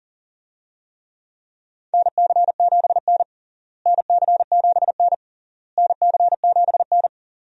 Morse code, heard as a keyed tone, about 40 wpm, 700 Hz. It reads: NC7N NC7N NC7N
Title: USA Call Signs - 4 Characters - Repeated 3x - 40wpm